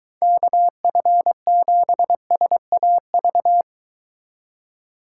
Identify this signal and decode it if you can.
Morse code, heard as a keyed tone, about 22 wpm, 700 Hz. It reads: KF7SAV